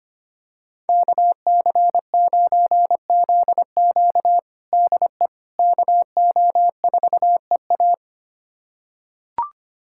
Morse code, heard as a keyed tone, about 25 wpm, 700 Hz. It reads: KC9ZQ DE KO4EA
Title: USA Callsign DE Callsign - 5 Characters - 25wpm